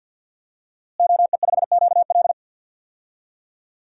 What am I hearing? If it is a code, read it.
Morse code, heard as a keyed tone, about 52 wpm, 700 Hz. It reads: OE5QL
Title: DXCC-100 Call Signs - 5 Characters - Repeated 2x - 50wpm